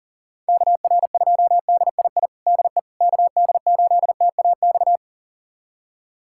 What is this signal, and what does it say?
Morse code, heard as a keyed tone, about 40 wpm, 700 Hz. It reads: KR2DII DE KD8TAX
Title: USA Callsign DE Callsign 40wpm